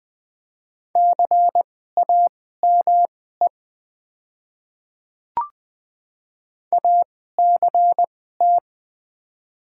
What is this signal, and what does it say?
Morse code, heard as a keyed tone, about 20 wpm, spaced as for 15 wpm, 700 Hz. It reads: CAME ACT
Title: 2x Character Spacing - Top 200 Words - 20wpm